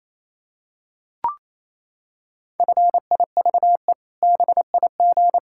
Morse code, heard as a keyed tone, about 28 wpm, 700 Hz. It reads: FIVE BIG
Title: Pangram Challenge v2 - 28wpm